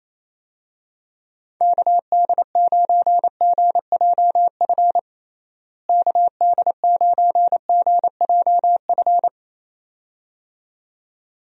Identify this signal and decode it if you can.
Morse code, heard as a keyed tone, about 27 wpm, 700 Hz. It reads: KD9GJF KD9GJF
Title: USA Call Signs - 6 Characters - Repeated 2x - 28wpm